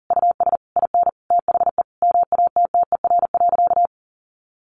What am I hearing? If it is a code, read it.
Morse code, heard as a keyed tone, about 40 wpm, 700 Hz. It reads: US IN THE MATTER.
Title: Sentences from Top 1200 Words 40wpm